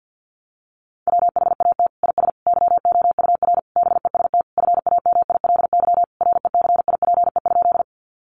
Morse code, heard as a keyed tone, about 50 wpm, 700 Hz. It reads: WHAT IS YOUR BEST FAMILY RECIPE?
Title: Sentences from Top 1200 Words 50wpm